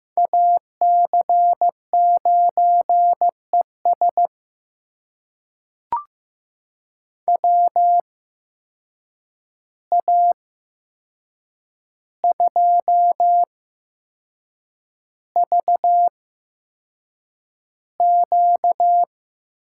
Morse code, heard as a keyed tone, about 15 wpm, spaced as for 6 wpm, 700 Hz. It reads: AC9ES WA2VQ
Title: USA Call Signs - 8x Character Spacing - 15wpm